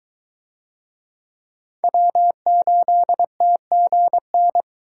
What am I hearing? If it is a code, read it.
Morse code, heard as a keyed tone, about 22 wpm, 700 Hz. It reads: W8TGN